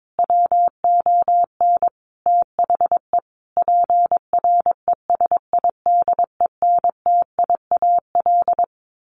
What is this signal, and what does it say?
Morse code, heard as a keyed tone, about 22 wpm, 700 Hz. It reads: WON THE PRESIDENTIAL
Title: A Sound Of Thunder 22wpm